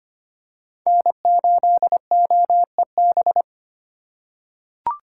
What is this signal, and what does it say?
Morse code, heard as a keyed tone, about 25 wpm, 700 Hz. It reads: N8OEB
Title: USA Call Signs - 5 Characters - 4x Character Spacing - 25wpm